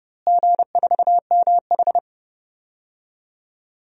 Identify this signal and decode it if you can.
Morse code, heard as a keyed tone, about 30 wpm, 700 Hz. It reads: G4MH
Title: DXCC-100 Call Signs - 4 Characters - Repeated 2x - 30wpm